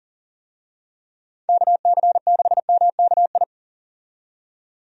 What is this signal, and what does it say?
Morse code, heard as a keyed tone, about 40 wpm, 700 Hz. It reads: KC6MKI
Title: USA Call Signs - 6 Characters - 40wpm